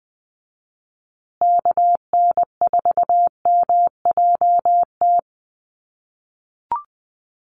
Morse code, heard as a keyed tone, about 20 wpm, 700 Hz. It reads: KN4MJT
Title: USA Call Signs Repeated 3x 20wpm